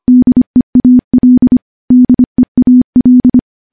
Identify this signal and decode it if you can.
Morse code, heard as a keyed tone, about 25 wpm, 250 Hz. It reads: DEAL DEAL
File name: morse-training-25WPM